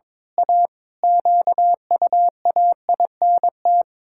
Morse code, heard as a keyed tone, about 22 wpm, 700 Hz. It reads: A QUAINT
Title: Pangram Challenge v2 - 22wpm